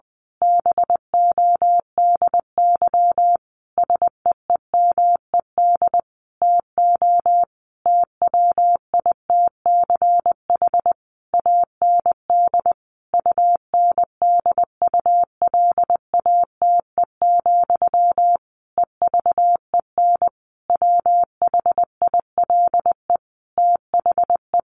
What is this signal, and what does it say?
Morse code, heard as a keyed tone, about 19 wpm, 700 Hz. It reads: BODY SEEMED TO TWITCH AND UNDULATE, EVEN WHILE THE
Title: A Sound Of Thunder 20wpm